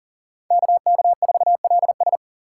Morse code, heard as a keyed tone, about 39 wpm, 700 Hz. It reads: KK4LS
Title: USA Call Signs - 3x Character Spacing - 40wpm